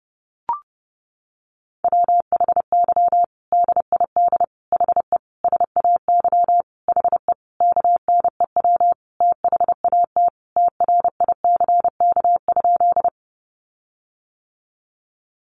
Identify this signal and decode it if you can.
Morse code, heard as a keyed tone, about 30 wpm, 700 Hz. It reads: WHY DID HE SAY HE KNEW THAT TRICK?
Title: Sentences from Top 1500 Words 30wpm